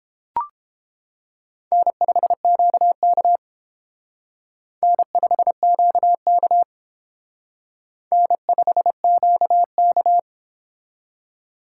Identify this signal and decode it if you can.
Morse code, heard as a keyed tone, about 28 wpm, 700 Hz. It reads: N5QK N5QK N5QK